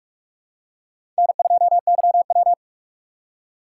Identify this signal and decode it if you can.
Morse code, heard as a keyed tone, about 46 wpm, 700 Hz. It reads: N2YW
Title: USA Call Signs - 2x Character Spacing - 45wpm